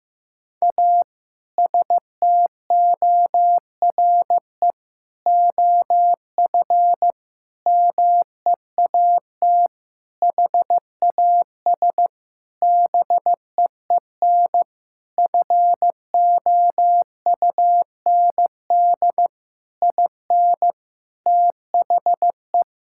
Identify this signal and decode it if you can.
Morse code, heard as a keyed tone, about 15 wpm, 700 Hz. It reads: A STORE OF MEAT HAS BEEN FOUND IN THE
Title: Sentences from Top 1000 Words - Review Part 5 - 15wpm